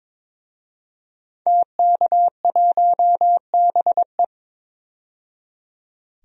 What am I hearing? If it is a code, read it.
Morse code, heard as a keyed tone, about 21 wpm, 700 Hz. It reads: TK1BE
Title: DXCC-100 Call Signs - 5 Characters - 22wpm